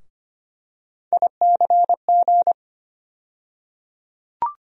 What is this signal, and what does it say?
Morse code, heard as a keyed tone, about 25 wpm, 700 Hz. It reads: ICG
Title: USA Call Sign Postfixes - 3 Letters 25wpm